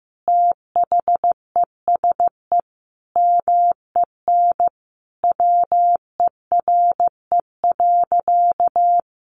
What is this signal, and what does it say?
Morse code, heard as a keyed tone, about 15 wpm, 700 Hz. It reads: THESE MEN WERE.
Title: Sentences from Top 1000 Words 15wpm